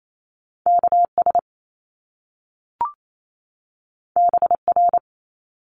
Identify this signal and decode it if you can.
Morse code, heard as a keyed tone, about 28 wpm, 700 Hz. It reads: KS BR
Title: Top Two Letter Combinations in Words 28wpm